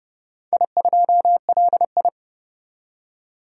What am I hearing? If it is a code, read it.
Morse code, heard as a keyed tone, about 30 wpm, 700 Hz. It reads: I2LI
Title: DXCC-100 Call Signs - 4 Characters - 30wpm